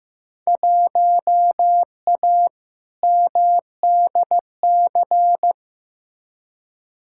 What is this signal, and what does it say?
Morse code, heard as a keyed tone, about 15 wpm, 700 Hz. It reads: 1A MDC
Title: Field Day Exchanges - 15wpm